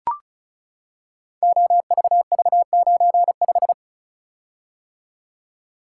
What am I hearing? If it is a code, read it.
Morse code, heard as a keyed tone, about 36 wpm, 700 Hz. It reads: OVV95
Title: Code Groups - 5 Letter:Numbers - 35wpm